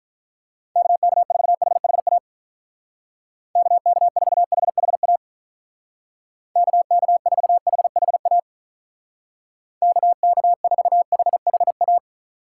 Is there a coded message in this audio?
Yes: Morse code from 0.8 to 12.0 s, heard as a keyed tone, about 45 wpm, 700 Hz. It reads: KK4HHA KK4HHA KK4HHA KK4HHA